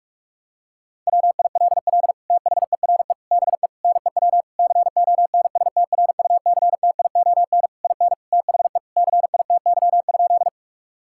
Morse code, heard as a keyed tone, about 46 wpm, 700 Hz. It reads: WILL THERE BE NEW CONSTRUCTION IN THE CITY?
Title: Sentences from Top 1600 Words 45wpm